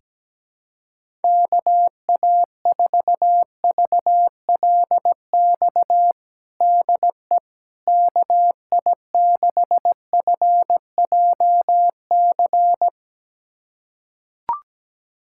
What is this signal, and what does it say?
Morse code, heard as a keyed tone, about 17 wpm, 700 Hz. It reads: KA4VLX DE KI6FJC
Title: USA Callsign DE Callsign - 6 Characters - 17wpm